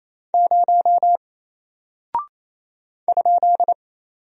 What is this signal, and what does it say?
Morse code, heard as a keyed tone, about 27 wpm, 700 Hz. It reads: 0 ?
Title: Single Character (Letter:Number:Punctuation:Prosign) 28wpm